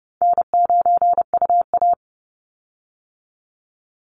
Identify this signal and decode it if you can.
Morse code, heard as a keyed tone, about 30 wpm, 700 Hz. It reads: N9UA
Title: Easy Call Signs Repeated 3x - 30wpm